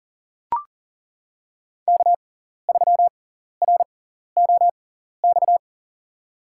Morse code, heard as a keyed tone, about 40 wpm, spaced as for 14 wpm, 700 Hz. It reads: K3ROX
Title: USA Call Signs - 5 Characters - 6x Character Spacing - 40wpm